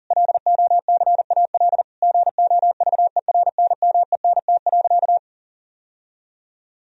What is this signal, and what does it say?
Morse code, heard as a keyed tone, about 40 wpm, 700 Hz. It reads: LOCAL GOVERNMENT.
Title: Sentences from Top 1000 Words - Review Part 3 - 40wpm